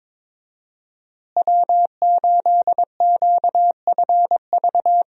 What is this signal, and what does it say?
Morse code, heard as a keyed tone, about 22 wpm, 700 Hz. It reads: W8QFV
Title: USA Call Signs - 5 Characters - Repeated 3x - 22wpm